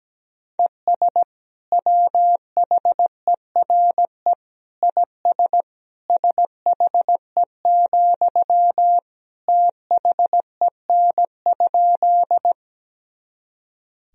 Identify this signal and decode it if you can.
Morse code, heard as a keyed tone, about 17 wpm, 700 Hz. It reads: ES WHERE IS SHE, THEN?
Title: Sentences from Top 200 Words and Abbreviations 17wpm